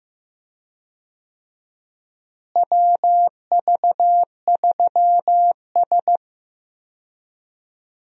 Morse code, heard as a keyed tone, about 15 wpm, 700 Hz. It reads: WV3S
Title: USA Call Signs - 4 Characters - Repeated 3x - 15wpm